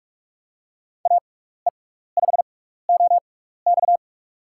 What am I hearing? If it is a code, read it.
Morse code, heard as a keyed tone, about 46 wpm, spaced as for 16 wpm, 700 Hz. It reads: AE5OX
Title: USA Call Signs - 6x Character Spacing - 45wpm